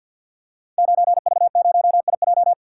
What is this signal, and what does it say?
Morse code, heard as a keyed tone, about 49 wpm, 700 Hz. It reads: 9V0IJ
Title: Code Groups - 5 Letter:Numbers - 50wpm